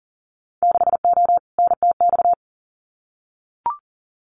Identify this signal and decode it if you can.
Morse code, heard as a keyed tone, about 40 wpm, 700 Hz. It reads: 6O NTX
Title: Winter Field Day Exchanges Encore 40wpm